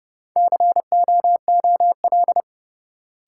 Morse code, heard as a keyed tone, about 30 wpm, 700 Hz. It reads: COOL
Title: Sentences from Top 1200 Words - New Word Drill - 30wpm